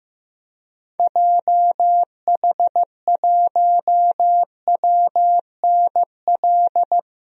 Morse code, heard as a keyed tone, about 15 wpm, 700 Hz. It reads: JH1WNL
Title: DXCC-100 Call Signs - 6 Characters - Repeated 3x - 15wpm